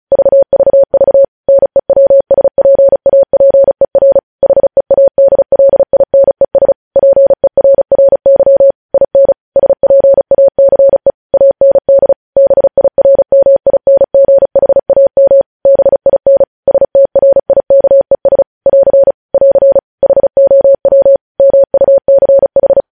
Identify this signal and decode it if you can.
Morse code, heard as a keyed tone, about 36 wpm, 550 Hz. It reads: VVV NEWSPAPER HEADLINES PERRY IN SPACE AND BIRMINGHAM BIN STRIKES <AR> <AR> HOW MUCH
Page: news headlines in morse code at 35 wpm